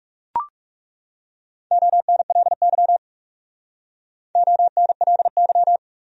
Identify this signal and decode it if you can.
Morse code, heard as a keyed tone, about 37 wpm, 700 Hz. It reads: ONLY ONLY
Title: Speed-Racing Top 500 Words 30wpm